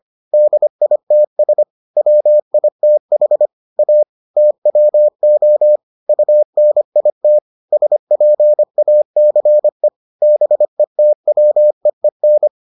Morse code, heard as a keyed tone, about 25 wpm, 600 Hz. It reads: DITS WITH A TWO UNIT SPACE BETWEEN